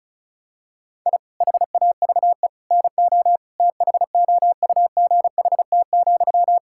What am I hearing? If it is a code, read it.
Morse code, heard as a keyed tone, about 34 wpm, 700 Hz. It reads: I HAVE NO THOUGHT,
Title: Sentences from Top 1000 Words - Review Part 6 - 35wpm